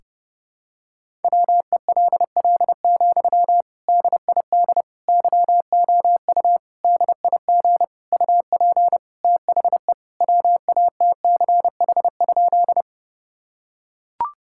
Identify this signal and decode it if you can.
Morse code, heard as a keyed tone, about 30 wpm, 700 Hz. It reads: WELL, DID YOU DIG UP THE WATCH?
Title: Sentences from Top 1600 Words 30wpm